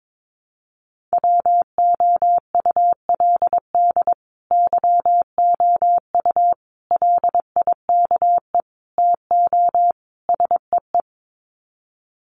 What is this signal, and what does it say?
Morse code, heard as a keyed tone, about 22 wpm, 700 Hz. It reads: WOULD YOU LIKE TO SEE
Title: Sets of 5 Words Encore2 22wpm